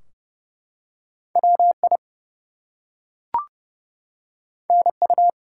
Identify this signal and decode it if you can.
Morse code, heard as a keyed tone, about 30 wpm, 700 Hz. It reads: WI NU
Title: US States and Canadian Provinces 30wpm